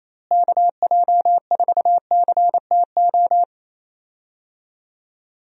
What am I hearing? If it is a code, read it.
Morse code, heard as a keyed tone, about 28 wpm, 700 Hz. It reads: KJ4CTO
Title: USA Call Signs - 6 Characters - Repeated 2x - 28wpm